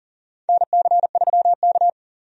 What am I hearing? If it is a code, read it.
Morse code, heard as a keyed tone, about 40 wpm, 700 Hz. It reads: NC3K
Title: USA Call Signs - 4 Characters - 40wpm